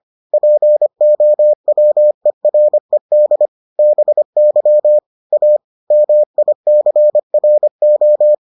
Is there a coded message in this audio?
Yes: Morse code from 0.3 to 8.3 s, heard as a keyed tone, about 25 wpm, 600 Hz. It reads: POWERED BY A MICRO